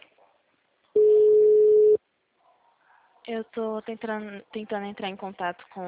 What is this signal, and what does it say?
Ringback tone (Europe, 425 Hz), with speech.
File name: out- -2015- - - .2